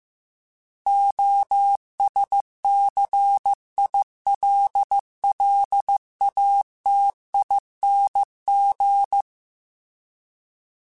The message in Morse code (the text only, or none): OSCILLATING